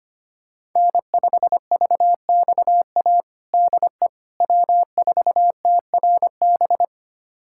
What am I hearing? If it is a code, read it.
Morse code, heard as a keyed tone, about 25 wpm, 700 Hz. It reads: N5VXA DE W4TRB
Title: USA Callsign DE Callsign - 5 Characters - 25wpm